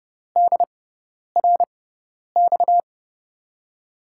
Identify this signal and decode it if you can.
Morse code, heard as a keyed tone, about 32 wpm, spaced as for 10 wpm, 700 Hz. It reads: DRX
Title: USA Call Signs - 6x Character Spacing - 30wpm